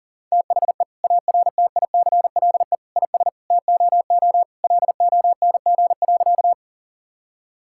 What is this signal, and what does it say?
Morse code, heard as a keyed tone, about 40 wpm, 700 Hz. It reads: THE ARTICLE IS TOO LONG.
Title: Sentences from Top 1000 Words - Review Part 5 - 40wpm